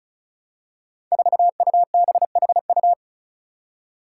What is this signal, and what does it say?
Morse code, heard as a keyed tone, about 36 wpm, 700 Hz. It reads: 4UBHU
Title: Five Challenging Letter-Number Combinations 35wpm